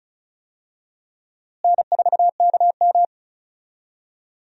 Morse code, heard as a keyed tone, about 36 wpm, 700 Hz. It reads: N4KM
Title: USA Call Signs - 4 Characters - Repeated 2x - 35wpm